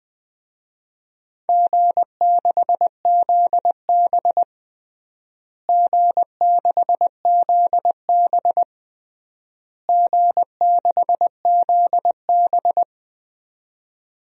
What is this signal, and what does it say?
Morse code, heard as a keyed tone, about 21 wpm, 700 Hz. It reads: G6ZB G6ZB G6ZB
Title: DXCC-100 Call Signs - 4 Characters - Repeated 3x - 20wpm